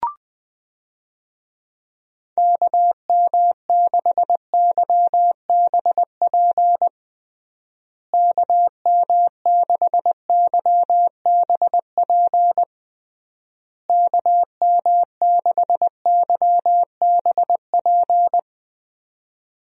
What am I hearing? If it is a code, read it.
Morse code, heard as a keyed tone, about 21 wpm, 700 Hz. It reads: KM6YBP KM6YBP KM6YBP
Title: USA Call Signs - 6 Characters - Repeated 3x - 20wpm